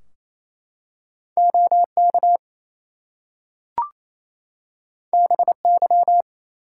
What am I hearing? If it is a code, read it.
Morse code, heard as a keyed tone, about 27 wpm, 700 Hz. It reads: OK BY
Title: Two Letter Words - 28wpm